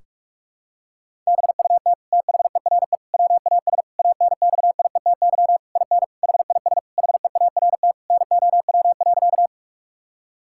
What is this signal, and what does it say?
Morse code, heard as a keyed tone, about 45 wpm, 700 Hz. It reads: BUT THERE WAS ANXIETY IN HIS HEART NOW.
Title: Sentences from Top 1600 Words 45wpm